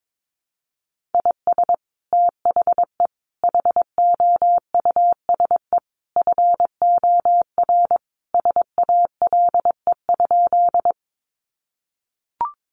Morse code, heard as a keyed tone, about 22 wpm, 700 Hz. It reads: IS THE HOUSE FOR SALE?
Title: Sentences from Top 1400 Words 22wpm